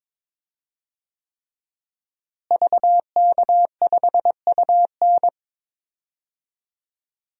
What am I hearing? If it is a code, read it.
Morse code, heard as a keyed tone, about 22 wpm, 700 Hz. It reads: VK5UN
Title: DXCC-100 Call Signs - 5 Characters - Repeated 3x - 22wpm